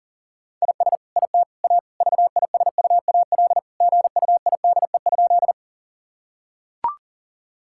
Morse code, heard as a keyed tone, about 40 wpm, 700 Hz. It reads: IS IT A VISUAL GUIDE?
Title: Sentences from Top 1000 Words - Review Part 6 - 40wpm